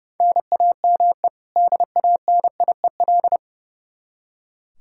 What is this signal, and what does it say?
Morse code, heard as a keyed tone, about 30 wpm, 700 Hz. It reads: NAME DANIEL
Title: Practice QSO - Exchange 1 - Beginning - 30wpm